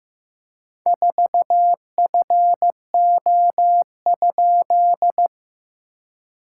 Morse code, heard as a keyed tone, about 15 wpm, 700 Hz. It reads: 4FO?
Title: SOTA QSO Practice 15wpm